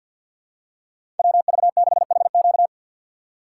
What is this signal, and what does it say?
Morse code, heard as a keyed tone, about 50 wpm, 700 Hz. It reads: WV6HQ
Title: USA Call Signs - 5 Characters - 2x Character Spacing - 50wpm